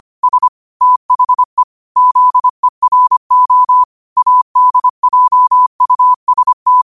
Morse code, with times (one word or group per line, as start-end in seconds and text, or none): 0.2-0.5 S
0.8-1.6 THE
2.0-3.8 ZERO
4.2-6.8 ADJUST